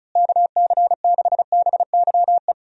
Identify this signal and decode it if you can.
Morse code, heard as a keyed tone, about 34 wpm, 700 Hz. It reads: KC6BYE
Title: USA Call Signs - 6 Characters - Repeated 3x - 35wpm